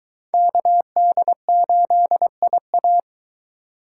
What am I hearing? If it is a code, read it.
Morse code, heard as a keyed tone, about 23 wpm, 700 Hz. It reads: KD8IA